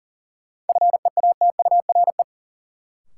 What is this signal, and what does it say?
Morse code, heard as a keyed tone, about 40 wpm, 700 Hz. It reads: FEATURE
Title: Sentences from Top 800 Words - New Word Drill - 40wpm